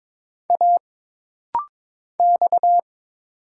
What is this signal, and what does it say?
Morse code, heard as a keyed tone, about 22 wpm, 700 Hz. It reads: A X
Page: This rapid-fire timing is geared to push your ICR (Instant Character Recognition) skills.